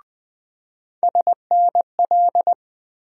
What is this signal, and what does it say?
Morse code, heard as a keyed tone, about 20 wpm, 700 Hz. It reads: SNL
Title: USA Call Sign Postfixes - 3 Letters 20wpm